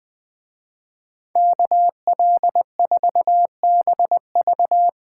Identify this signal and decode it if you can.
Morse code, heard as a keyed tone, about 20 wpm, 700 Hz. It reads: KL4BV